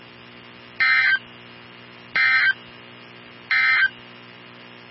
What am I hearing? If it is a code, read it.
Emergency Alert System (SAME) tones, heard as harsh buzzing bursts. It is heard from 0.7 to 4.0 s.
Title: NOAA Weather Radio 162.550